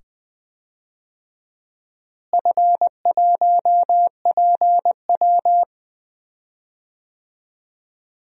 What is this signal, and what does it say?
Morse code, heard as a keyed tone, about 20 wpm, 700 Hz. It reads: F1PW
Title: DXCC-100 Call Signs - 4 Characters - Repeated 2x - 20wpm